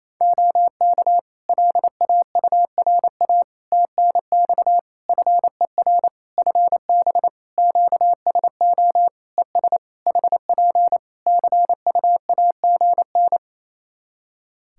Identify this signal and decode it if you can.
Morse code, heard as a keyed tone, about 27 wpm, 700 Hz. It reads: OK LAURA TNX FER FB QSO ES HP CUAGN
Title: K1USN SST QSO Practice 28wpm